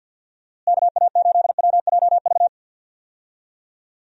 Morse code, heard as a keyed tone, about 52 wpm, 700 Hz. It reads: KA8WJV
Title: USA Call Signs - 6 Characters - Repeated 2x - 50wpm